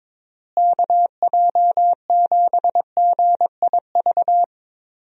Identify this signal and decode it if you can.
Morse code, heard as a keyed tone, about 22 wpm, 700 Hz. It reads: KJ7GIV
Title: USA Call Signs - 6 Characters - Repeated 2x - 22wpm